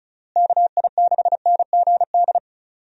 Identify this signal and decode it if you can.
Morse code, heard as a keyed tone, about 36 wpm, 700 Hz. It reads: KI6NGD
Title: USA Call Signs - 6 Characters - Repeated 3x - 35wpm